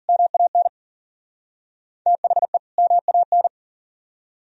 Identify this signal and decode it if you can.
Morse code, heard as a keyed tone, about 50 wpm, 700 Hz. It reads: MAN THE MAN